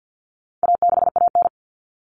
Morse code, heard as a keyed tone, about 50 wpm, 700 Hz. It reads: A6AN
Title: Common Cut Numbers - 4-Digit 50wpm